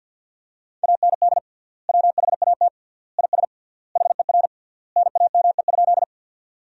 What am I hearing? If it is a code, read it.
Morse code, heard as a keyed tone, about 50 wpm, 700 Hz. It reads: AND WHAT IS HER NAME?
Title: 3x Word-Spacing - Sentences from Top 200 Words - 50wpm